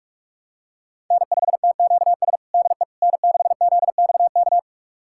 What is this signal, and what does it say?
Morse code, heard as a keyed tone, about 46 wpm, 700 Hz. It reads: N5TQS DE N6ZXK
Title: USA Callsign DE Callsign - 5 Characters - 45wpm